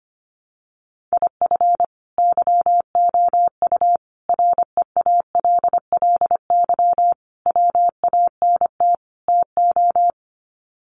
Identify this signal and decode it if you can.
Morse code, heard as a keyed tone, about 25 wpm, 700 Hz. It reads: IF YOU REALLY WANT TO
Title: Sets of 5 Words Encore2 25wpm